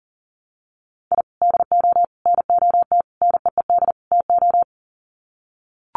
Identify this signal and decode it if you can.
Morse code, heard as a keyed tone, about 40 wpm, 700 Hz. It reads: I DO NOT NEED TO
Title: Sets of 5 Words 40wpm